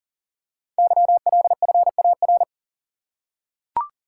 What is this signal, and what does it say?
Morse code, heard as a keyed tone, about 38 wpm, 690 Hz. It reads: YLFAR
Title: Code Groups - 5 Letters - 40wpm